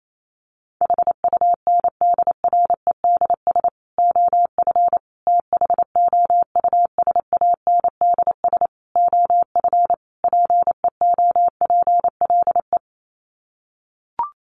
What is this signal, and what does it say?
Morse code, heard as a keyed tone, about 28 wpm, 700 Hz. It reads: HUNDREDS OF THOUSANDS OF PEOPLE
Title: Sets of 5 Words Encore2 28wpm